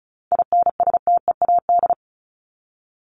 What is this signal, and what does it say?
Morse code, heard as a keyed tone, about 34 wpm, 700 Hz. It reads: INSTEAD
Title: Top 501-750 Words 35wpm